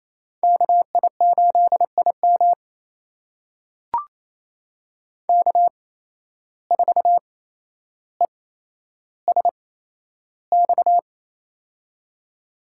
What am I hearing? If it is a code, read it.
Morse code, heard as a keyed tone, about 28 wpm, spaced as for 12 wpm, 700 Hz. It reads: KI8IM K4ESX
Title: USA Call Signs - 5 Characters - 8x Character Spacing - 28wpm